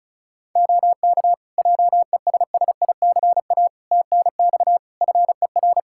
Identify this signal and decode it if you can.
Morse code, heard as a keyed tone, about 36 wpm, 700 Hz. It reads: OK JESSICA TNX FER
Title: Practice QSO - Exchange 4 - Ending - 35wpm